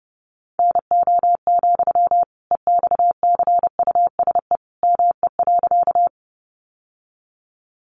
Morse code, heard as a keyed tone, about 30 wpm, 700 Hz. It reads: NO, EXCUSE ME.
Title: Sentences from Top 700 Words 30wpm